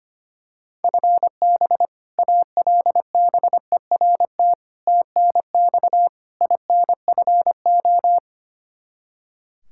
Morse code, heard as a keyed tone, about 25 wpm, 700 Hz. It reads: FB ALBERT TNX INFO
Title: Practice QSO - Exchange 3 - More Info - 25wpm